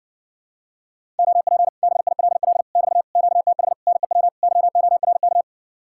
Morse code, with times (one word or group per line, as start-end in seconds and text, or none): 1.2-1.7 OP
1.8-2.6 BILL
2.8-3.0 <BT>
3.2-3.7 QTH
3.9-4.3 NEW
4.4-5.4 YORK